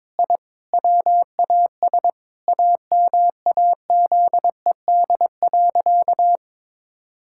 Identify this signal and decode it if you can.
Morse code, heard as a keyed tone, about 21 wpm, 700 Hz. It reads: I WAS AMAZED.
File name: practice / 22wpm / Sentences with 4-Word Phrases - 22wpm